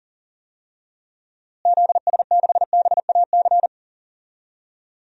Morse code, heard as a keyed tone, about 41 wpm, 700 Hz. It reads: ZS6BAC
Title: DXCC-100 Call Signs - 6 Characters - Repeated 3x - 40wpm